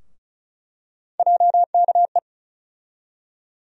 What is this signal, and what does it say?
Morse code, heard as a keyed tone, about 35 wpm, 700 Hz. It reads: JKE
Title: USA Call Sign Postfixes - 3 Letters 35wpm